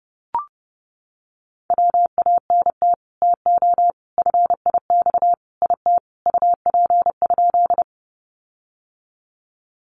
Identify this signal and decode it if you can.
Morse code, heard as a keyed tone, about 30 wpm, 700 Hz. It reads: WANT TO FIX IT UP?
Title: Sentences from Top 1400 Words 30wpm